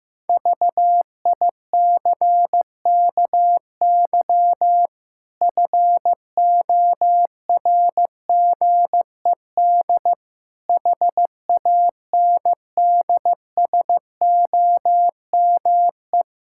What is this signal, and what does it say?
Morse code, heard as a keyed tone, about 15 wpm, 700 Hz. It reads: VICKY FORGED HANDSOME
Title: Pangram Challenge v2 - 15wpm